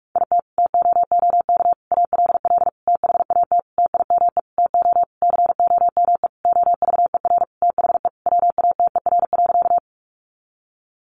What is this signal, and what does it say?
Morse code, heard as a keyed tone, about 44 wpm, 700 Hz. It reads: IT TOOK ALL THAT TIME TO COME OVER THE WATER.
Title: Sentences from Top 300 Words 45wpm